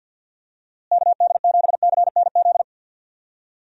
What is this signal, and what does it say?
Morse code, heard as a keyed tone, about 52 wpm, 700 Hz. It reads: KD7CNZ
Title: USA Call Signs - 6 Characters - Repeated 2x - 50wpm